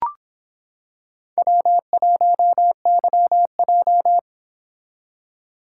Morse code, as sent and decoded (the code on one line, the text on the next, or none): .-- .---- -.-- .---
W1YJ